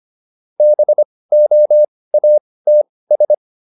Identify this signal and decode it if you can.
Morse code, heard as a keyed tone, about 25 wpm, spaced as for 18 wpm, 600 Hz. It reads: BOATS